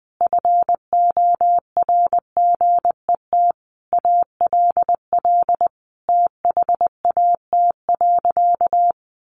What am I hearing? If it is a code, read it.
Morse code, heard as a keyed tone, about 20 wpm, 700 Hz. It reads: FORGET ALL THAT.
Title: Sentences from Top 1200 Words 20wpm